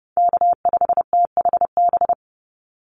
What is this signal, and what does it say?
Morse code, heard as a keyed tone, about 30 wpm, 700 Hz. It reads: K5THB
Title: USA Call Signs 30wpm